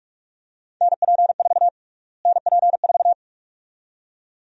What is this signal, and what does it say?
Morse code, heard as a keyed tone, about 46 wpm, 700 Hz. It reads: NP4 NP4
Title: USA Call Sign Prefixes - Repeated 2x - 45wpm